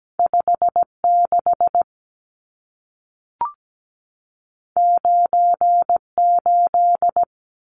Morse code, heard as a keyed tone, about 17 wpm, 700 Hz. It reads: 56 98
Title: Numbers - 2-Digit 17wpm